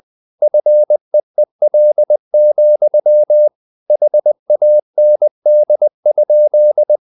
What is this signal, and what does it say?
Morse code, heard as a keyed tone, about 20 wpm, 600 Hz. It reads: FEEL, HAND?
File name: the old man and the sea / the old man and the sea part 1 20wpm